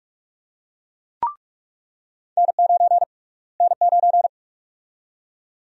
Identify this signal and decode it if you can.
Morse code, heard as a keyed tone, about 42 wpm, 700 Hz. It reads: N9 N9
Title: USA Call Sign Prefixes - Repeated 2x - 45wpm